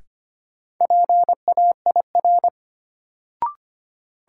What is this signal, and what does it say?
Morse code, heard as a keyed tone, about 25 wpm, 700 Hz. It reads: PAIR
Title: Sentences from Top 1200 Words - New Word Drill Alt - 25wpm